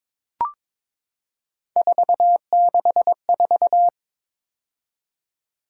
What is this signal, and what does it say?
Morse code, heard as a keyed tone, about 23 wpm, 700 Hz. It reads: 464
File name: practice / 22wpm / Three Letter-Number Combinations - 22wpm